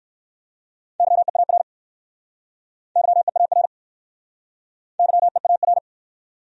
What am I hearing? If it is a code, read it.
Morse code, heard as a keyed tone, about 59 wpm, 700 Hz. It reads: YEAR YEAR YEAR